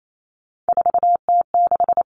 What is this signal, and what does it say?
Morse code, heard as a keyed tone, about 28 wpm, 700 Hz. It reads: 4T6
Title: Common Cut Numbers - 3-Digit 28wpm